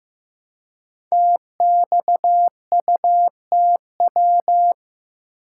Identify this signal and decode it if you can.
Morse code, heard as a keyed tone, about 15 wpm, 700 Hz. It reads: TXUTW
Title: Code Groups - 5 Letters - 15wpm